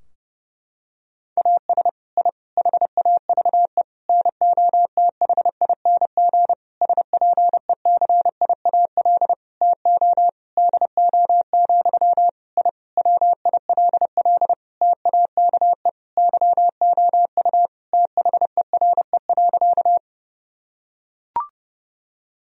Morse code, heard as a keyed tone, about 30 wpm, 700 Hz. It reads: AS I HAVE NOTHING SPECIAL TO DO, I WILL TAKE YOU THERE.
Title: Sentences from Top 1000 Words - Review Part 3 - 30wpm